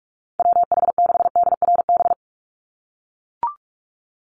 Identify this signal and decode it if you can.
Morse code, heard as a keyed tone, about 44 wpm, 700 Hz. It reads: WH6DRB
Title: USA Call Signs 45wpm